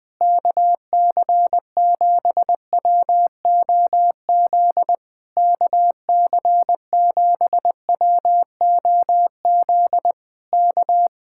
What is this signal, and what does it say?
Morse code, heard as a keyed tone, about 20 wpm, 700 Hz. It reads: KC7WOZ KC7WOZ K